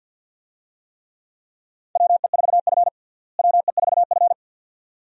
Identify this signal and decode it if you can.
Morse code, heard as a keyed tone, about 50 wpm, 700 Hz. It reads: WE4F WE4F
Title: USA Call Signs - 4 Characters - Repeated 2x - 50wpm